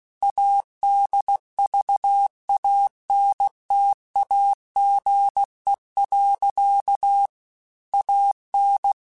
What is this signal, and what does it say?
Morse code, heard as a keyed tone, about 16 wpm, 800 Hz. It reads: ADVANTAGE. AN
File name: Technical Subjects / Technical Subjects-mp3s / Sw Pwr Sup @15